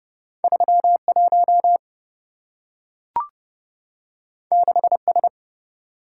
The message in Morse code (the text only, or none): 31 6S